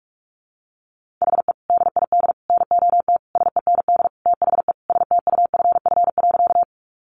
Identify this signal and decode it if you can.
Morse code, heard as a keyed tone, about 44 wpm, 700 Hz. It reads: HE DID NOT SEND THE STUFF.
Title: Sentences from Top 1500 Words 45wpm